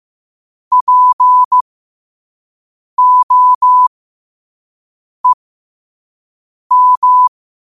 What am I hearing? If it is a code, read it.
Morse code, heard as a keyed tone, about 15 wpm, spaced as for 5 wpm, 1000 Hz. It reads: POEM
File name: qotdfiles / 5wpm